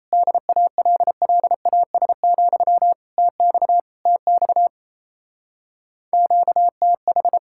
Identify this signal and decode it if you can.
Morse code, heard as a keyed tone, about 32 wpm, 700 Hz. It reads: DALLAS, TX TX QTH
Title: Speed-Racing Common QSO Phrases - 28wpm